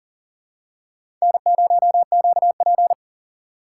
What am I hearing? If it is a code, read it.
Morse code, heard as a keyed tone, about 40 wpm, 700 Hz. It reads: N0QP
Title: USA Call Signs - 4 Characters - 2x Character Spacing - 40wpm